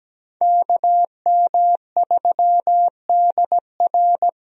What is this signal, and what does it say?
Morse code, heard as a keyed tone, about 17 wpm, 700 Hz. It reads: KM3DR
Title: USA Call Signs - 5 Characters - Repeated 3x - 17wpm